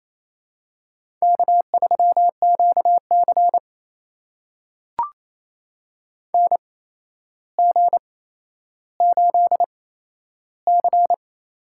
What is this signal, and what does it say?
Morse code, heard as a keyed tone, about 28 wpm, spaced as for 12 wpm, 700 Hz. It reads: K3QC NG8C
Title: USA Call Signs - 4 Characters - 8x Character Spacing - 28wpm